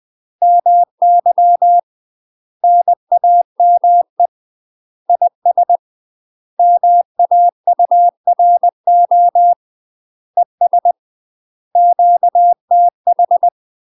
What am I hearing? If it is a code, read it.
Morse code, heard as a keyed tone, about 20 wpm, 700 Hz. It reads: MY NAME IS MAURO ES QTH
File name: UK MASTER QSOs TRAINING / 20WPM / UK MASTER QSOs 0036